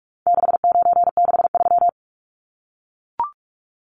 Morse code, heard as a keyed tone, about 44 wpm, 700 Hz. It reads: 6963
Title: Numbers - 4-Digit 45wpm